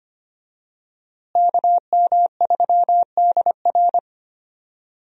Morse code, heard as a keyed tone, about 25 wpm, 700 Hz. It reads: KM3DR
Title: USA Call Signs - 5 Characters - Repeated 3x - 25wpm